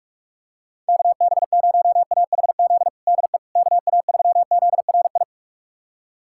Morse code, heard as a keyed tone, about 42 wpm, 700 Hz. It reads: KB0AHZ DE KA3ZRI
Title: USA Callsign DE Callsign - 6 Characters - 45wpm